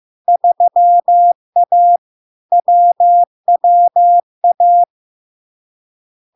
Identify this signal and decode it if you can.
Morse code, heard as a keyed tone, about 15 wpm, 700 Hz. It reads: 3A WWA
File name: LARK-FD-2022-Log-in-CW-15wpm-700-Hz